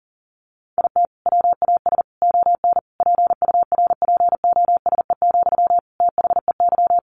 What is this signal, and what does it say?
Morse code, heard as a keyed tone, about 40 wpm, 700 Hz. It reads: IT WAS ON PURPOSE, THEY
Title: Sentences from Top 800 Words 40wpm